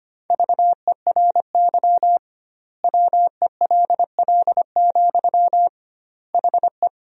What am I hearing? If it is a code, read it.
Morse code, heard as a keyed tone, about 25 wpm, 700 Hz. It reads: VERY WELL, HE
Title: 2x Word-Spacing - Sentences from Top 200 Words - 25wpm